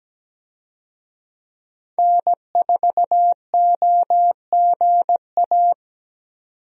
Morse code, heard as a keyed tone, about 17 wpm, 700 Hz. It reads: N4OGA
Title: USA Call Signs - 5 Characters - 17wpm